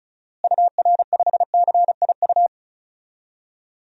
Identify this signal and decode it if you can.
Morse code, heard as a keyed tone, about 36 wpm, 700 Hz. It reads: UR5CIU
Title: DXCC-100 Call Signs - 6 Characters - Repeated 3x - 35wpm